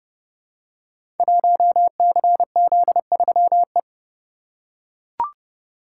Morse code, heard as a keyed tone, about 30 wpm, 700 Hz. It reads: 1CZ3E
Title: Code Groups - 5 Letter:Numbers - 30wpm